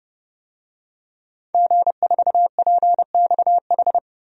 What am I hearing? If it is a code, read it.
Morse code, heard as a keyed tone, about 29 wpm, 700 Hz. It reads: G4PXH
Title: DXCC-100 Call Signs - 5 Characters - 30wpm